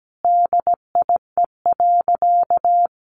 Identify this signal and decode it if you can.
Morse code, heard as a keyed tone, about 17 wpm, 700 Hz. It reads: DIE.
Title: Sentences from Top 1300 Words 17wpm